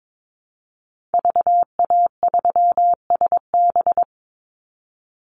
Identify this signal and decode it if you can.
Morse code, heard as a keyed tone, about 22 wpm, 700 Hz. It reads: VA3SB
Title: Call Signs Repeated 2x - 22wpm